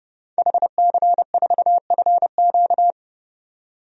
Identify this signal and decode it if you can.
Morse code, heard as a keyed tone, about 31 wpm, 700 Hz. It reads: HC4FQ
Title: DXCC-100 Call Signs - 5 Characters - 30wpm